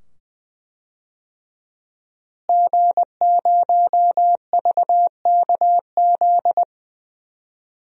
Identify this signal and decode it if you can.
Morse code, heard as a keyed tone, about 20 wpm, 700 Hz. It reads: G0VKZ
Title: DXCC-100 Call Signs - 5 Characters - Repeated 2x - 20wpm